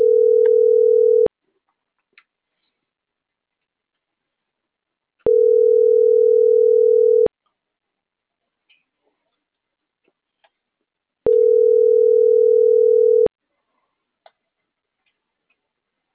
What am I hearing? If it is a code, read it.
Ringback tone (North America), 440 + 480 Hz.
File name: CENTRAL MAERSK / out luis.herrera - .84777